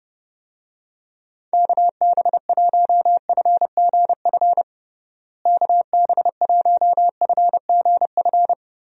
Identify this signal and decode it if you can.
Morse code, heard as a keyed tone, about 30 wpm, 700 Hz. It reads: KB1FGF KB1FGF
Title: USA Call Signs - 6 Characters - Repeated 2x - 30wpm